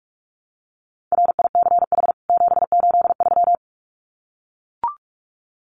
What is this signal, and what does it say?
Morse code, heard as a keyed tone, about 44 wpm, 700 Hz. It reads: RICH 783
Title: CWT Exchange 45wpm